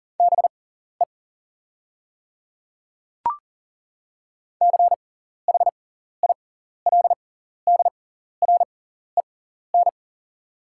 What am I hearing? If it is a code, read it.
Morse code, heard as a keyed tone, about 40 wpm, spaced as for 14 wpm, 700 Hz. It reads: BE CHILDREN
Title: 6x Character Spacing - Top 500 Words - 40wpm